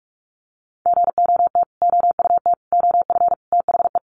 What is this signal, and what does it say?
Morse code, heard as a keyed tone, about 48 wpm, 700 Hz. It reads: GOT OUT OF THE
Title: Sets of 5 Words Encore 45wpm